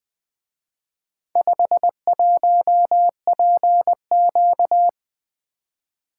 Morse code, heard as a keyed tone, about 24 wpm, 700 Hz. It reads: 51PQ
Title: Speed-Racing - Four Letter-Number Combinations - 20wpm